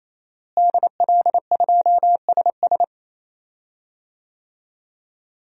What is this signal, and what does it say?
Morse code, heard as a keyed tone, about 28 wpm, 700 Hz. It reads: DL2SS
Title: DXCC-100 Call Signs - 5 Characters - Repeated 3x - 28wpm